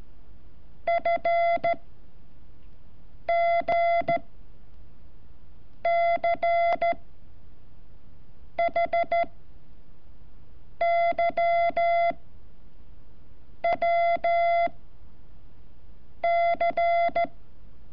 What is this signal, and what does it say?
Morse code, heard as a keyed tone, about 14 wpm, 680 Hz. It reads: F G C H Y W C
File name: curso / ejercicio 28